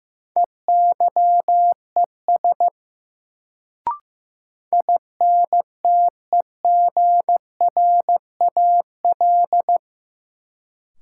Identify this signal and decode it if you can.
Morse code, heard as a keyed tone, about 16 wpm, 700 Hz. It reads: EYES INTEGRAL
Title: Sets of 5 Words - New Word Drill - 15wpm